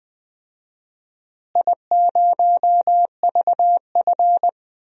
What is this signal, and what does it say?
Morse code, heard as a keyed tone, about 20 wpm, 700 Hz. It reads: I0VF
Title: DXCC-100 Call Signs - 4 Characters - Repeated 3x - 20wpm